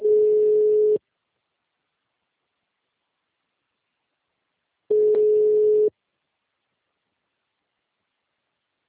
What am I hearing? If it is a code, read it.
Ringback tone (Europe, 425 Hz).
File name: out- -2017- - - .84